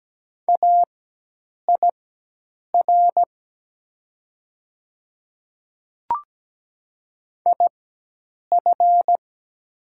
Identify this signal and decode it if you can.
Morse code, heard as a keyed tone, about 17 wpm, spaced as for 8 wpm, 700 Hz. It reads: AIR IF
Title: 4x Character Spacing - Top 300 Words - 17wpm